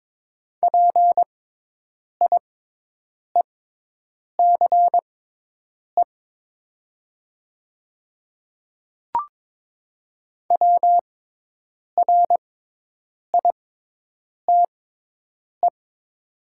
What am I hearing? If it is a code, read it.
Morse code, heard as a keyed tone, about 22 wpm, spaced as for 8 wpm, 700 Hz. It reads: PIECE WRITE
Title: 6x Character Spacing - Top 500 Words - 22wpm